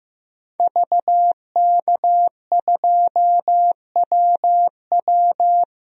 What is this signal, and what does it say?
Morse code, heard as a keyed tone, about 15 wpm, 700 Hz. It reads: VK2WW
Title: DXCC-100 Call Signs - 5 Characters - 15wpm